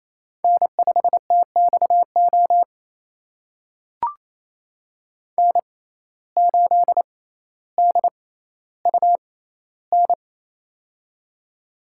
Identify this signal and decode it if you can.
Morse code, heard as a keyed tone, about 28 wpm, spaced as for 14 wpm, 700 Hz. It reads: N5TXO N8DUN
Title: USA Call Signs - 5 Characters - 6x Character Spacing - 28wpm